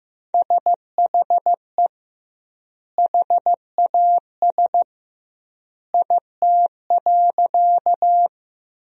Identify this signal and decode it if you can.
Morse code, heard as a keyed tone, about 15 wpm, 700 Hz. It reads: SHE HAS IT.
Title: 2x Word-Spacing - Sentences from Top 100 Words - 15wpm